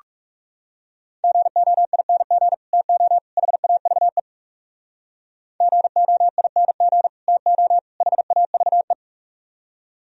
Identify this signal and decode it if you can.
Morse code, heard as a keyed tone, about 44 wpm, 700 Hz. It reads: GOING TO HAVE GOING TO HAVE